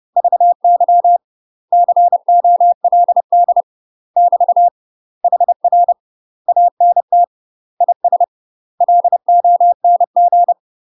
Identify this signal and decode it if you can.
Morse code, heard as a keyed tone, about 30 wpm, 700 Hz. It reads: VY COLD <BT> HR ANT IS LONG
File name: UK MASTER QSOs TRAINING / 30WPM / UK MASTER QSOs 0001